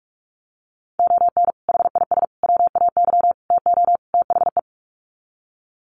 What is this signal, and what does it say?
Morse code, heard as a keyed tone, about 44 wpm, 690 Hz. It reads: ON HIS WAY TO THE
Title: Sets of 5 Words Encore2 45wpm